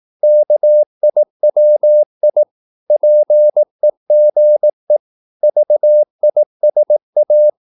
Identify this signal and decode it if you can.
Morse code, heard as a keyed tone, about 18 wpm, 600 Hz. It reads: KIWI PEGE VISA
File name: Korte ord DK 18wpm